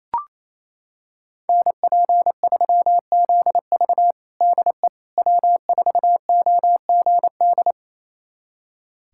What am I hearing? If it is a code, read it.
Morse code, heard as a keyed tone, about 27 wpm, 700 Hz. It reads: NP3ZV DE W4OGD
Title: USA Callsign DE Callsign - 5 Characters - 28wpm